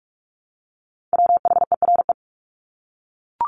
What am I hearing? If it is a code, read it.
Morse code, heard as a keyed tone, about 44 wpm, 700 Hz. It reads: WHERE
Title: Top 300 Words 45wpm